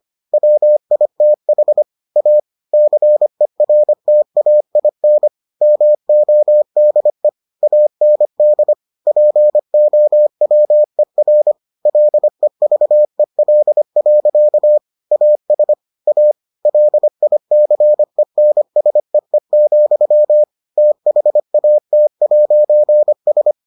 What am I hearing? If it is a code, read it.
Morse code, heard as a keyed tone, about 25 wpm, 600 Hz. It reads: WITH A CERTAIN MODE AND POWER LEVEL. AS A LICENSEE, THAT'S